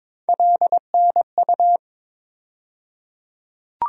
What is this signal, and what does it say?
Morse code, heard as a keyed tone, about 22 wpm, 700 Hz. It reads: LNU
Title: USA Call Sign Postfixes - 3 Letters 22wpm